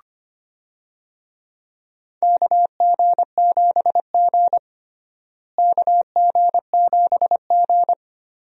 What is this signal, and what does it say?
Morse code, heard as a keyed tone, about 25 wpm, 700 Hz. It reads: KG7G KG7G
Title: USA Call Signs - 4 Characters - Repeated 2x - 25wpm